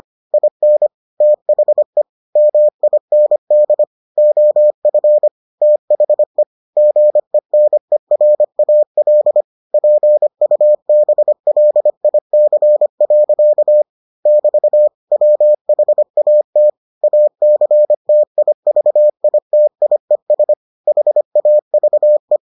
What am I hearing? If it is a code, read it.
Morse code, heard as a keyed tone, about 25 wpm, 600 Hz. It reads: IN THE MIND OF THE GENERAL PUBLIC. <BT> WHAT ACTIVITIES HAVE